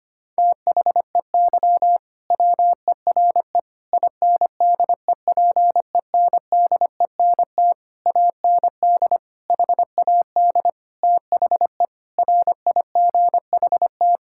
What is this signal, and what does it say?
Morse code, heard as a keyed tone, about 25 wpm, 700 Hz. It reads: THEY WERE INDEPENDENT AND HAD THE RIGHT
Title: Sentences from Top 1000 Words - Review Part 6 - 25wpm